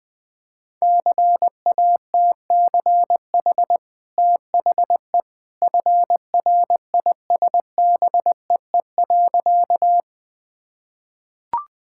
Morse code, heard as a keyed tone, about 20 wpm, 700 Hz. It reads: CATCH THE FRISBEE.
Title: Easy Sentences with Hints - 20wpm